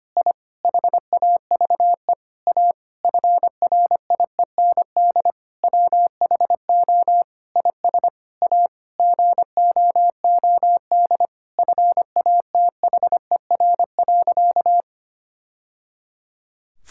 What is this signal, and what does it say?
Morse code, heard as a keyed tone, about 25 wpm, 700 Hz. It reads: I HAVE A FRIEND WHO IS A GOOD FATHER.
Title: Sentences from Top 1000 Words - Review Part 5 - 25wpm